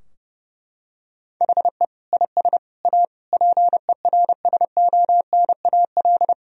Morse code, heard as a keyed tone, about 30 wpm, 700 Hz. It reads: HE IS A PERSONAL
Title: Sentences from Top 1000 Words - Review Part 4 - 30wpm